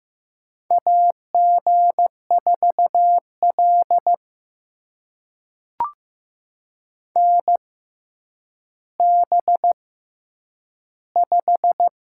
Morse code, heard as a keyed tone, about 15 wpm, spaced as for 9 wpm, 700 Hz. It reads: AG4L NB5
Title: USA Call Signs - 4 Characters - 6x Character Spacing - 15wpm